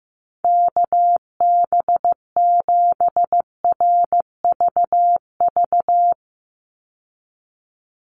Morse code, heard as a keyed tone, about 15 wpm, 700 Hz. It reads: KB7RVV
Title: USA Call Signs 15wpm